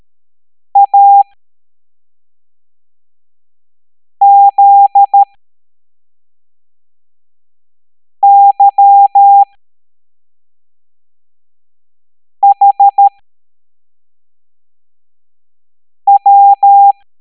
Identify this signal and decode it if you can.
Morse code, heard as a keyed tone, about 13 wpm, spaced as for 3 wpm, 800 Hz. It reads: AZYHW